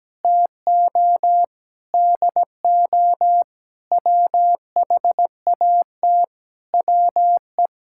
Morse code, heard as a keyed tone, about 17 wpm, 700 Hz. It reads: TO DO WHAT WE
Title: Sentences from Top 1600 Words 17wpm